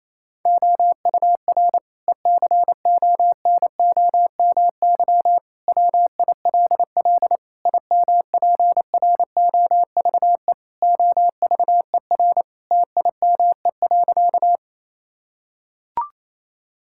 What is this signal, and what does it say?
Morse code, heard as a keyed tone, about 28 wpm, 700 Hz. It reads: OUR ECONOMY WILL IMPROVE OVER TIME.
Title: Sentences from Top 1000 Words - Review Part 5 - 28wpm